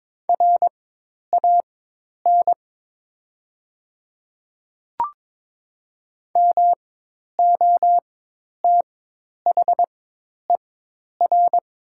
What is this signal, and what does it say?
Morse code, heard as a keyed tone, about 22 wpm, spaced as for 10 wpm, 700 Hz. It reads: RAN MOTHER
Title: 4x Character Spacing - Top 500 Words - 22wpm